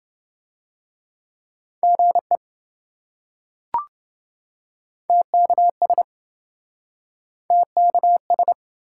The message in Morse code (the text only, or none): GE TKS TKS